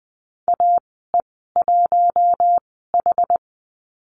Morse code, heard as a keyed tone, about 20 wpm, spaced as for 15 wpm, 700 Hz. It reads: AE1H
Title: 062 Lesson 11g - Letter h - Call Signs 20wpm